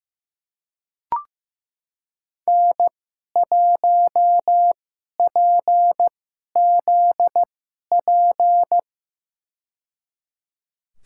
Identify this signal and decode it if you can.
Morse code, heard as a keyed tone, about 15 wpm, spaced as for 11 wpm, 700 Hz. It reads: N1PZP
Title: USA Call Signs - 5 Characters - 2x Character Spacing - 15wpm